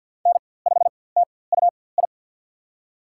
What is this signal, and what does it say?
Morse code, heard as a keyed tone, about 53 wpm, spaced as for 24 wpm, 700 Hz. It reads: N5TUI
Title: USA Call Signs - 4x Character Spacing - 50wpm